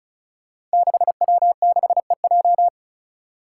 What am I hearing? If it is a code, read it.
Morse code, heard as a keyed tone, about 36 wpm, 700 Hz. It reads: 6W6EJ
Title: Five Challenging Letter-Number Combinations 35wpm